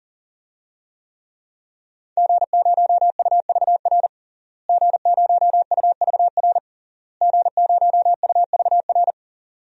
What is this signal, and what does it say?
Morse code, heard as a keyed tone, about 37 wpm, 700 Hz. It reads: G0UVR G0UVR G0UVR
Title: DXCC-100 Call Signs - 5 Characters - Repeated 3x - 40wpm